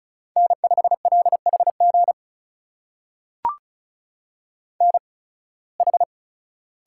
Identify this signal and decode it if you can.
Morse code, heard as a keyed tone, about 36 wpm, spaced as for 23 wpm, 700 Hz. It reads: N5LHG NH
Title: USA Call Signs - 5 Characters - 8x Character Spacing - 35wpm